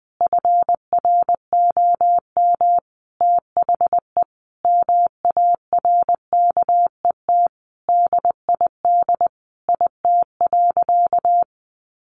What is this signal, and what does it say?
Morse code, heard as a keyed tone, about 20 wpm, 700 Hz. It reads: FROM THE MARKET DID IT.
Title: Sentences from Top 600 Words 20wpm